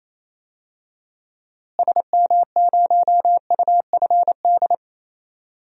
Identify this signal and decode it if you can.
Morse code, heard as a keyed tone, about 28 wpm, 700 Hz. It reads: SM0UFD
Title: DXCC-100 Call Signs - 6 Characters - Repeated 3x - 28wpm